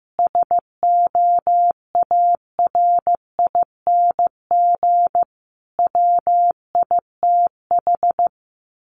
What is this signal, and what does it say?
Morse code, heard as a keyed tone, about 15 wpm, 700 Hz. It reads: SOARING WITH
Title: A Sound Of Thunder 15wpm